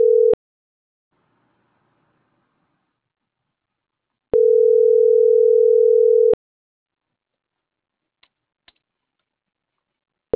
Ringback tone (North America), 440 + 480 Hz.